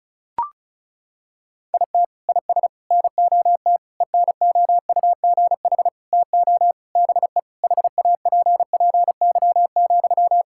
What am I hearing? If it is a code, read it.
Morse code, heard as a keyed tone, about 36 wpm, 700 Hz. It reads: IT IS NOT ENOUGH TO BE HAPPY,
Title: Sentences from Top 1000 Words - Review Part 6 - 35wpm